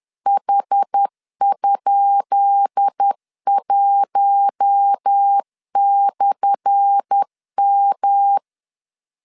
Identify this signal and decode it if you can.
Morse code, heard as a keyed tone, about 10 wpm, 800 Hz. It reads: H?1/M
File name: LEZIONI / 10 SIGNS / 10B INT 10wpm 2844